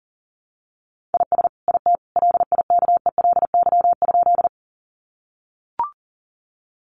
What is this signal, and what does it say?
Morse code, heard as a keyed tone, about 40 wpm, 700 Hz. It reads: IS IT LIKELY?
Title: Sentences from Top 600 Words 40wpm